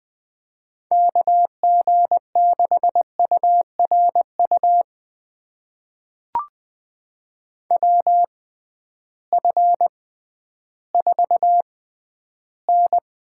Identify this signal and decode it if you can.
Morse code, heard as a keyed tone, about 20 wpm, spaced as for 12 wpm, 700 Hz. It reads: KG6URU WF4N
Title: USA Call Signs - 6x Character Spacing - 20wpm